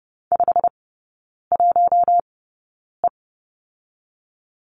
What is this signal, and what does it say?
Morse code, heard as a keyed tone, about 30 wpm, spaced as for 9 wpm, 700 Hz. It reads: 51E
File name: course / 30:10wpm / 087 Lesson 15e - Number 5 - Three Characters 30wpm